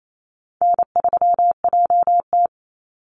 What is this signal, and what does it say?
Morse code, heard as a keyed tone, about 28 wpm, 700 Hz. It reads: N3JT
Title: Easy Call Signs Repeated 2x - 28wpm